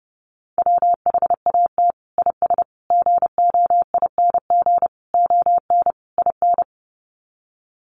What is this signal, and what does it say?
Morse code, heard as a keyed tone, about 30 wpm, 700 Hz. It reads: WHAT IS GOING ON IN
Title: Sets of 5 Words Encore2 30wpm